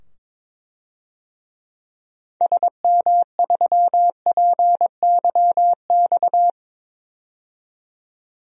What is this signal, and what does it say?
Morse code, heard as a keyed tone, about 22 wpm, 700 Hz. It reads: SM3PYX
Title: DXCC-100 Call Signs - 6 Characters - 22wpm